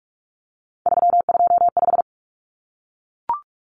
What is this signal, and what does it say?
Morse code, heard as a keyed tone, about 44 wpm, 700 Hz. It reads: 325
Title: Numbers - 3-Digit 45wpm